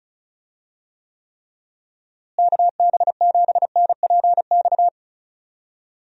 Morse code, heard as a keyed tone, about 34 wpm, 700 Hz. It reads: KB7NPX
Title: USA Call Signs - 6 Characters - Repeated 3x - 35wpm